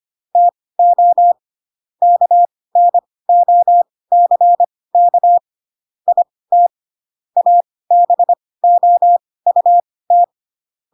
Morse code, heard as a keyed tone, about 25 wpm, spaced as for 18 wpm, 700 Hz. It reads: TO KNOCK IT ABOUT
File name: Pinocchio / Pinocchio ch1 12 25 ik1hge